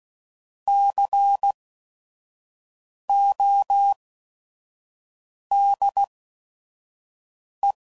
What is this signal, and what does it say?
Morse code, heard as a keyed tone, about 16 wpm, spaced as for 5 wpm, 800 Hz. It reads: CODE
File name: Prefixes / By First Letter / Cs